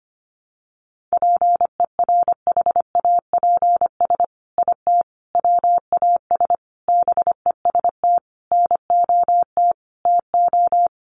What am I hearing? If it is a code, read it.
Morse code, heard as a keyed tone, about 25 wpm, 700 Hz. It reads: PERHAPS IT WAS BEST NOT TO
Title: Sentences from Top 600 Words 25wpm